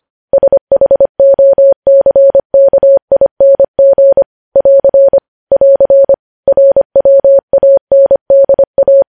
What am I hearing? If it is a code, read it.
Morse code, heard as a keyed tone, about 25 wpm, 550 Hz. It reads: SHOCKING <AR> <AR> RWANDA
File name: podcast / News headlines-1x-25x25